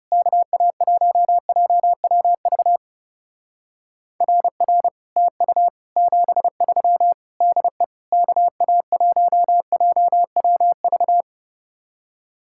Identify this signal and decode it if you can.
Morse code, heard as a keyed tone, about 39 wpm, 700 Hz. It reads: KA1JWV RR TU 73 DE KA1JWV